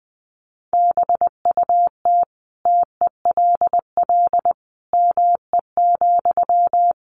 Morse code, heard as a keyed tone, about 19 wpm, 700 Hz. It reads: BUT TELL ME,
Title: Sentences from Top 1500 Words 20wpm